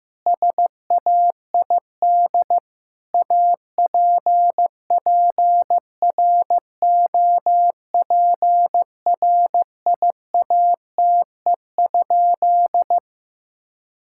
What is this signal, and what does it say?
Morse code, heard as a keyed tone, about 15 wpm, 700 Hz. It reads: SAID APPROPRIATE?
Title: Sentences from Top 1000 Words - Review Part 5 - 15wpm